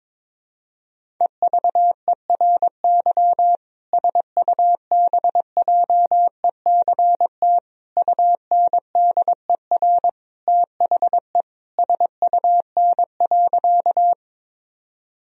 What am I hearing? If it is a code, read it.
Morse code, heard as a keyed tone, about 22 wpm, 700 Hz. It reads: EVERY SUBJECT UNDER THE SUN.
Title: Sentences from Top 1000 Words - Review Part 1 - 22wpm